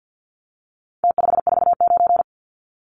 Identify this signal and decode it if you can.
Morse code, heard as a keyed tone, about 50 wpm, 700 Hz. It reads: T549
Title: Numbers - Contest Style - 4-Digit 50wpm